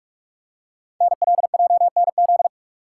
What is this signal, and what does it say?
Morse code, heard as a keyed tone, about 44 wpm, 700 Hz. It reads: NLJNZ
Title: Code Groups - 5 Letters - 45wpm